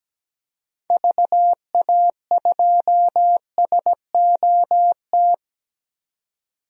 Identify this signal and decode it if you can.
Morse code, heard as a keyed tone, about 17 wpm, 700 Hz. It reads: VA2SOT
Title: DXCC-100 Call Signs - 6 Characters - 17wpm